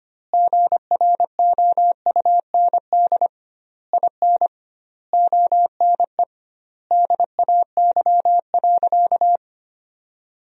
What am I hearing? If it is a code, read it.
Morse code, heard as a keyed tone, about 25 wpm, 700 Hz. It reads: GROUND IN ONE DAY.
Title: 2x Word-Spacing - Sentences from Top 500 Words - 25wpm